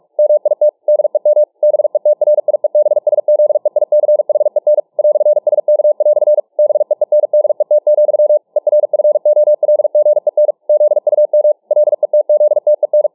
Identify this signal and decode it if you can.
Morse code, heard as a keyed tone, about 46 wpm, 600 Hz. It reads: MIT DEM BETRIEBSZEICHEN "SK" BEENDET, ERFOLGEN ZUM LETZTEN